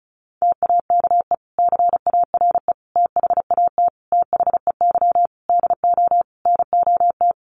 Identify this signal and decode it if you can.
Morse code, heard as a keyed tone, about 34 wpm, 700 Hz. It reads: TAKE CARE THAT THEY DO NOT
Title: Sentences from Top 1500 Words 35wpm